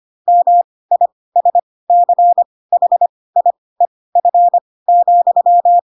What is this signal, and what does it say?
Morse code, heard as a keyed tone, about 25 wpm, spaced as for 18 wpm, 700 Hz. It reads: MISCHIEF,